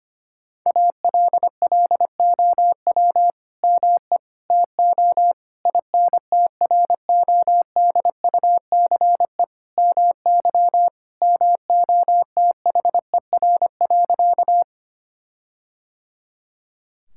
Morse code, heard as a keyed tone, about 25 wpm, 700 Hz. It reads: ALLOW ME TO INTRODUCE MY MOTHER.
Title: Sentences from Top 1600 Words 25wpm